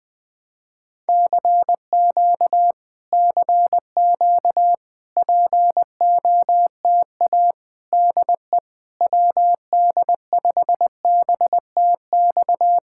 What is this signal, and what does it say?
Morse code, heard as a keyed tone, about 20 wpm, 700 Hz. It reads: CQ CQ POTA DE WD5BTX
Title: Speed-Racing Common QSO Phrases - 17wpm